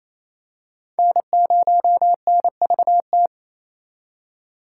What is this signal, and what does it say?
Morse code, heard as a keyed tone, about 28 wpm, 700 Hz. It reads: N0NVT
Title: USA Call Signs - 5 Characters - 2x Character Spacing - 28wpm